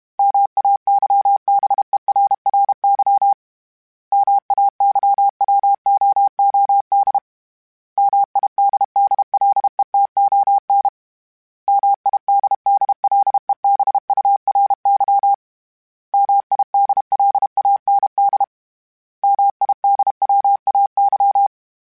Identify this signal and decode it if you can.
Morse code, heard as a keyed tone, about 32 wpm, 800 Hz. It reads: MAYBERRY MAYWOOD MIDDLETON MIDDLEBURY MIDLAND MIDWAY
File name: fromSMs / FromSMs mp3s / Cities@30